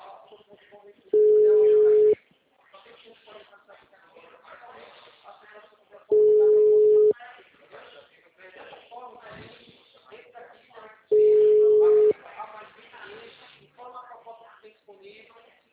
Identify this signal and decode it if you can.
Ringback tone (Europe, 425 Hz), with speech.